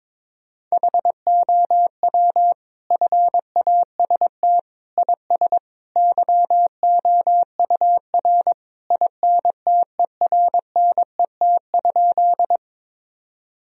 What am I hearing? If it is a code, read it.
Morse code, heard as a keyed tone, about 22 wpm, 700 Hz. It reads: HOW FAST IS YOUR INTERNET?
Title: Sentences from Top 1000 Words - Review Part 4 - 22wpm